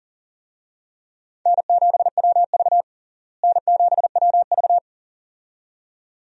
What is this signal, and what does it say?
Morse code, heard as a keyed tone, about 40 wpm, 700 Hz. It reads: N7WV N7WV
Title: USA Call Signs - 4 Characters - Repeated 2x - 40wpm